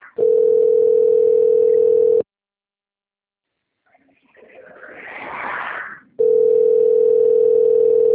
Ringback tone (North America), 440 + 480 Hz.
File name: rg-1600-unknown- - - .21923